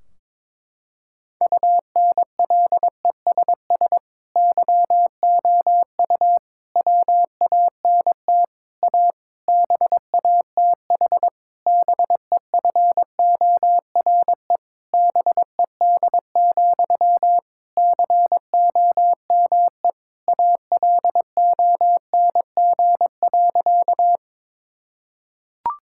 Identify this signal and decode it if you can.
Morse code, heard as a keyed tone, about 22 wpm, 700 Hz. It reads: UNLESS YOU WANT A BATH BEFORE BED, COME ALONG.
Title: Sentences from Top 1600 Words 22wpm